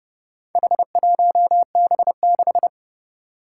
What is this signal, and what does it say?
Morse code, heard as a keyed tone, about 30 wpm, 700 Hz. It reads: H1B6
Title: Four Challenging Letter-Number Combinations 30wpm